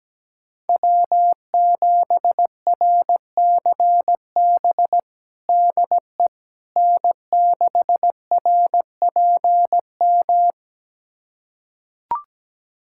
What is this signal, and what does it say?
Morse code, heard as a keyed tone, about 17 wpm, 700 Hz. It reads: W7RCB DE N6RPM
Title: USA Callsign DE Callsign - 5 Characters - 17wpm